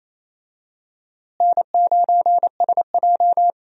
Morse code, heard as a keyed tone, about 28 wpm, 700 Hz. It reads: N9SJ
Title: USA Call Signs - 4 Characters - Repeated 3x - 28wpm